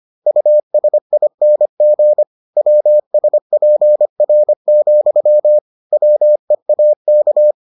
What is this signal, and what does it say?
Morse code, heard as a keyed tone, about 25 wpm, 600 Hz. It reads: USING WSPR, WEAK